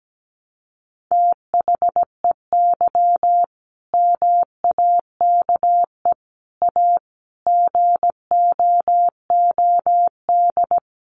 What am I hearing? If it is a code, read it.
Morse code, heard as a keyed tone, about 17 wpm, 700 Hz. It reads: THEY MAKE A GOOD
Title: Sentences from Top 1200 Words 17wpm